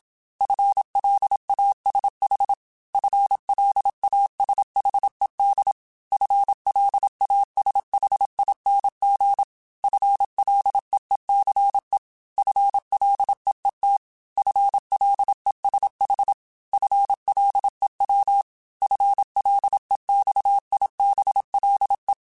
Morse code, heard as a keyed tone, about 27 wpm, 800 Hz. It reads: FLASH FLASHED FLASHING FLEECE FLEET FLESH FLEW FLEXIBLE